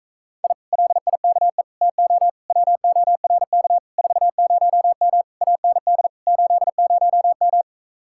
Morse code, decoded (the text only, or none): I LIKE TO WORK 40M AND 80M